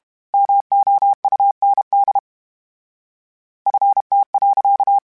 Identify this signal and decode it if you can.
Morse code, heard as a keyed tone, about 32 wpm, 800 Hz. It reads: MOUND FT.
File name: fromSMs / FromSMs mp3s / Cities@30